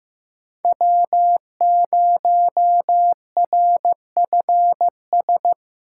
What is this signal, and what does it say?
Morse code, heard as a keyed tone, about 15 wpm, 700 Hz. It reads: W0RFS
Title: USA Call Signs - 5 Characters - 3x Character Spacing - 15wpm